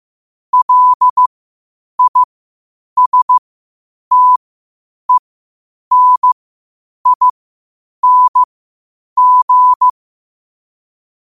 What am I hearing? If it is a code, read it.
Morse code, heard as a keyed tone, about 15 wpm, spaced as for 8 wpm, 1000 Hz. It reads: LISTENING
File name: qotdfiles / 8wpm